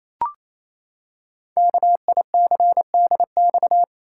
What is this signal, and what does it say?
Morse code, heard as a keyed tone, about 28 wpm, 700 Hz. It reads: KICDX
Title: Code Groups - 5 Letter:Numbers - 28wpm